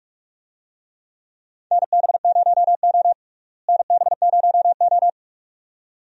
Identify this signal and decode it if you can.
Morse code, heard as a keyed tone, about 46 wpm, 700 Hz. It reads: NB0O NB0O
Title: USA Call Signs - 4 Characters - Repeated 2x - 45wpm